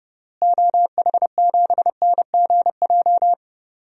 Morse code, heard as a keyed tone, about 31 wpm, 700 Hz. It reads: OH7NGJ
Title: DXCC-100 Call Signs - 6 Characters - Repeated 3x - 30wpm